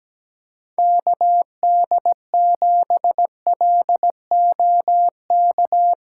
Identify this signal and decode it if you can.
Morse code, heard as a keyed tone, about 17 wpm, 700 Hz. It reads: KD7LOK
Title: USA Callsign DE Callsign 17wpm